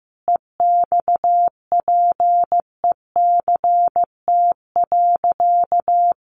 Morse code, heard as a keyed tone, about 15 wpm, 700 Hz. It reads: EXPECT.
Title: Sentences from Top 1100 Words 15wpm